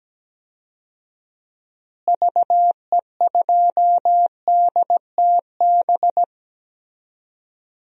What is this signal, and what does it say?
Morse code, heard as a keyed tone, about 17 wpm, 700 Hz. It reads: VE2DTB
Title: DXCC-100 Call Signs - 6 Characters - Repeated 3x - 17wpm